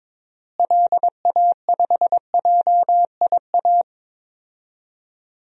Morse code, heard as a keyed tone, about 22 wpm, 700 Hz. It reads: LA5JIA
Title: DXCC-100 Call Signs - 6 Characters - Repeated 2x - 22wpm